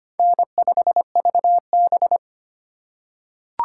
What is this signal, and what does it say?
Morse code, heard as a keyed tone, about 25 wpm, 700 Hz. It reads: N5VB
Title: USA Call Signs - 4 Characters - 2x Character Spacing - 25wpm